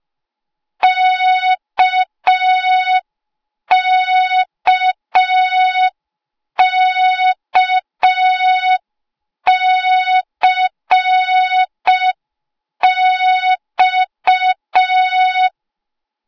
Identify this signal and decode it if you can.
Morse code, heard as a keyed tone, about 5 wpm, 3700 Hz. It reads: KKKCX
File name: Morze-HA2MN / hang / 08KXC HA2MN